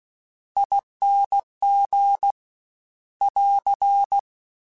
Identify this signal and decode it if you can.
Morse code, heard as a keyed tone, about 16 wpm, 800 Hz. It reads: ING <AR>
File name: Special INGs / Special INGs-mp3s / VerySlowToFaster Ing @15